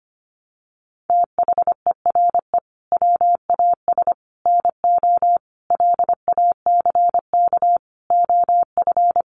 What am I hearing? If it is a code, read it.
Morse code, heard as a keyed tone, about 25 wpm, 700 Hz. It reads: THERE WAS NO LACK OF
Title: Sentences from Top 1100 Words 25wpm